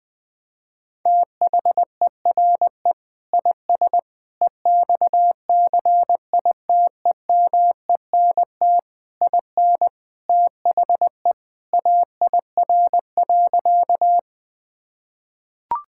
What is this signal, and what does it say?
Morse code, heard as a keyed tone, about 19 wpm, 700 Hz. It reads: THERE IS EXCITEMENT IN THE AIR.
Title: Sentences with 3-Word Phrases - 20wpm